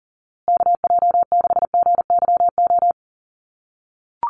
Morse code, heard as a keyed tone, about 40 wpm, 700 Hz. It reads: KJ6GYO
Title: USA Call Signs Repeated 2x 40wpm